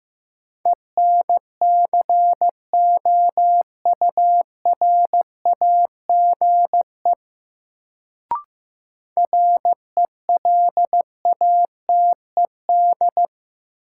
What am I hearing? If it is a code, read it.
Morse code, heard as a keyed tone, about 15 wpm, 700 Hz. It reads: ENCOURAGE RELATED
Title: Sentences from Top 1400 Words - New Word Drill Alt - 15wpm